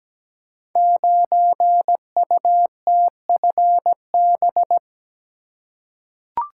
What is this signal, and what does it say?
Morse code, heard as a keyed tone, about 17 wpm, 700 Hz. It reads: 9UTFB
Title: Code Groups - 5 Letter:Numbers - 17wpm